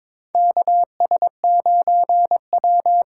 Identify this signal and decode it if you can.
Morse code, heard as a keyed tone, about 22 wpm, 700 Hz. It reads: KS9W
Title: K1USN SST QSO Practice - 22wpm